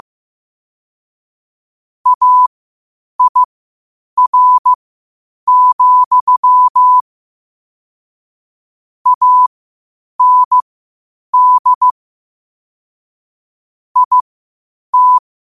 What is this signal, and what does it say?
Morse code, heard as a keyed tone, about 15 wpm, spaced as for 8 wpm, 1000 Hz. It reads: AIR, AND IT
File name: qotdfiles / 8wpm